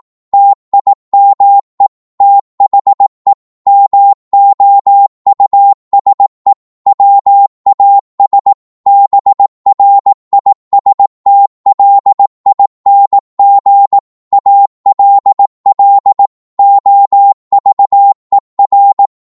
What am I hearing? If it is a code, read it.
Morse code, heard as a keyed tone, about 18 wpm, 800 Hz. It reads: TIME THE MOUSE WAS BRISTLING ALL OVER
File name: Morse / psg-18wpm